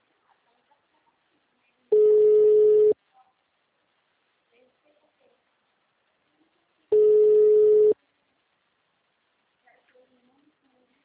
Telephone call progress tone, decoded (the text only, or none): ringback tone